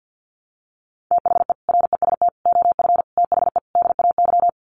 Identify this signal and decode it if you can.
Morse code, heard as a keyed tone, about 50 wpm, 700 Hz. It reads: THE REST OF THE DAY
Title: Sets of 5 Words 50wpm